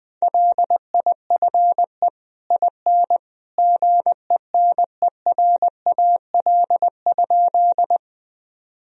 Morse code, heard as a keyed tone, about 20 wpm, 700 Hz. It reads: LIFE IN GENERAL?
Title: Sentences from Top 1000 Words - Review Part 3 - 20wpm